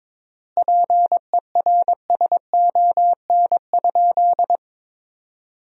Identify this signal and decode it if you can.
Morse code, heard as a keyed tone, about 22 wpm, 700 Hz. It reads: PERSON?
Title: 3x Word-Spacing - Sentences from Top 400 Words - 22wpm